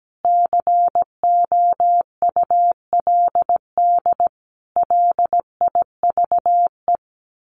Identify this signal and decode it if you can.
Morse code, heard as a keyed tone, about 17 wpm, 700 Hz. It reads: COULD LIVE
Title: Sentences from Top 1400 Words 17wpm